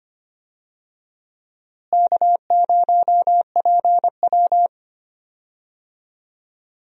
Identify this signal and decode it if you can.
Morse code, heard as a keyed tone, about 25 wpm, 700 Hz. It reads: K0PW
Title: USA Call Signs - 4 Characters - Repeated 3x - 25wpm